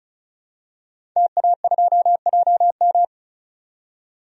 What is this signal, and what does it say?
Morse code, heard as a keyed tone, about 34 wpm, 700 Hz. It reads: TA2JM
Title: DXCC-100 Call Signs - 5 Characters - 35wpm